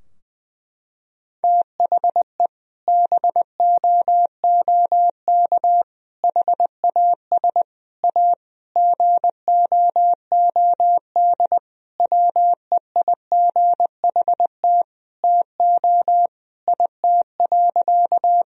Morse code, heard as a keyed tone, about 20 wpm, 700 Hz. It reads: THE BOOK HAS A GOOD WEIGHT TO IT.
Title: Sentences from Top 1000 Words - Review Part 3 - 20wpm